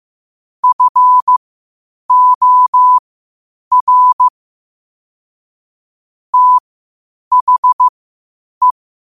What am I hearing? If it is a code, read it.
Morse code, heard as a keyed tone, about 15 wpm, spaced as for 8 wpm, 1000 Hz. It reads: FOR THE
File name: qotdfiles / 8wpm